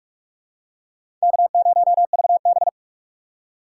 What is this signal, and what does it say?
Morse code, heard as a keyed tone, about 46 wpm, 700 Hz. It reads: K0VB
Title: USA Call Signs - 4 Characters - 4x Character Spacing - 45wpm